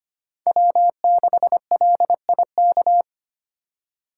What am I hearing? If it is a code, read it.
Morse code, heard as a keyed tone, about 25 wpm, 700 Hz. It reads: W6LIK
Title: USA Call Signs - 4x Character Spacing - 25wpm